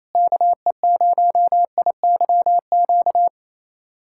Morse code, heard as a keyed tone, about 28 wpm, 700 Hz. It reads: KE0IYQ
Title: USA Callsign DE Callsign - 6 Characters - 28wpm